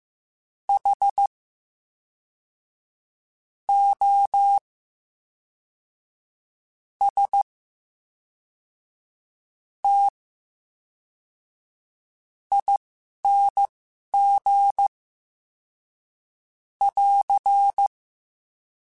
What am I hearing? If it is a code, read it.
Morse code, heard as a keyed tone, about 15 wpm, spaced as for 11 wpm, 800 Hz. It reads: H O S T ING <AR>
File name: Special INGs / Special INGs-mp3s / VerySlowToFaster Ing @10